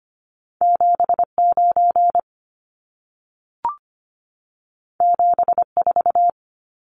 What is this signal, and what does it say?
Morse code, heard as a keyed tone, about 25 wpm, 700 Hz. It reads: 79 74
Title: Numbers - 2-Digit 25wpm